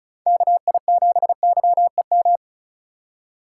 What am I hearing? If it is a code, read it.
Morse code, heard as a keyed tone, about 34 wpm, 700 Hz. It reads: KI7YEM
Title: USA Call Signs - 6 Characters - 2x Character Spacing - 35wpm